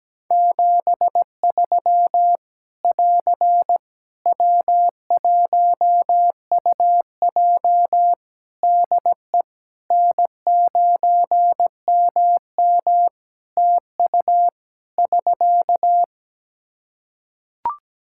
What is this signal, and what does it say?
Morse code, heard as a keyed tone, about 17 wpm, 700 Hz. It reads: 73 <AR> W1UJ DE N9MM TU <SK>
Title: Practice QSO - Exchange 4 - Ending - 17wpm